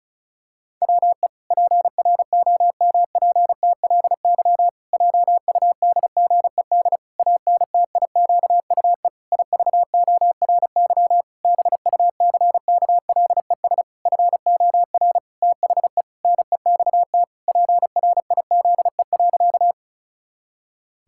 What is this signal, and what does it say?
Morse code, heard as a keyed tone, about 34 wpm, 700 Hz. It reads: WE PROMPTLY JUDGED ANTIQUE IVORY BUCKLES FOR THE NEXT PRIZE.
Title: Pangram Challenge v2 - 35wpm